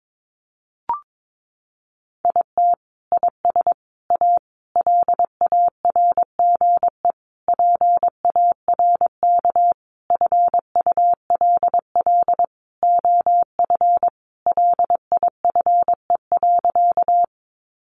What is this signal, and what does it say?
Morse code, heard as a keyed tone, about 22 wpm, 700 Hz. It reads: IT IS A LARGE PARK FULL OF LIFE.
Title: Sentences from Top 1300 Words 22wpm